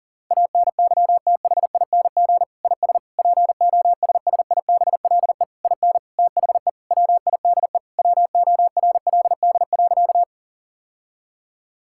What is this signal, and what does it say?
Morse code, heard as a keyed tone, about 40 wpm, 700 Hz. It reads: ANYTHING IS POSSIBLE IN THE WIDE WORLD.
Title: Sentences from Top 1000 Words - Review Part 6 - 40wpm